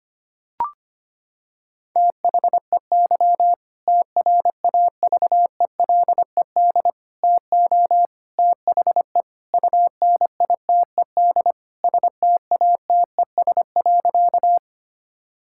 Morse code, heard as a keyed tone, about 25 wpm, 700 Hz. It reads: THEY TRAVELED TO THE UNITED STATES.
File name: practice / 25wpm / Sentences with 2-Word Phrases - 25wpm